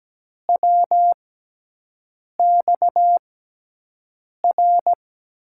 Morse code, heard as a keyed tone, about 17 wpm, spaced as for 6 wpm, 700 Hz. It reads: WXR
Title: USA Call Signs - 6 Characters - 6x Character Spacing - 17wpm